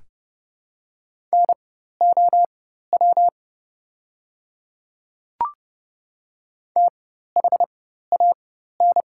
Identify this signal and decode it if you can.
Morse code, heard as a keyed tone, about 30 wpm, spaced as for 14 wpm, 700 Hz. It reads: NOW THAN
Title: 4x Character Spacing - Top 200 Words - 30wpm